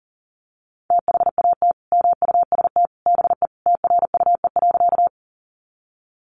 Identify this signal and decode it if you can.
Morse code, heard as a keyed tone, about 40 wpm, 700 Hz. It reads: THAT MUST BE TRUE.
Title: Sentences from Top 400 Words 40wpm